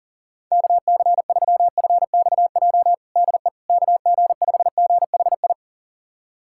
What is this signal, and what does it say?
Morse code, heard as a keyed tone, about 37 wpm, 700 Hz. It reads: KC3FXJ DE KG5GHI
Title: USA Callsign DE Callsign 40wpm